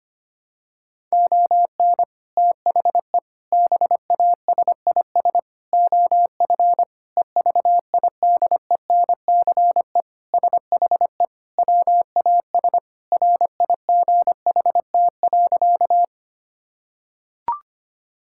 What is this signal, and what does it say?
Morse code, heard as a keyed tone, about 25 wpm, 700 Hz. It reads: ON THE BASIS OF EVIDENCE SHE WAS RIGHT.
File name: practice / 25wpm / Sentences with 4-Word Phrases - 25wpm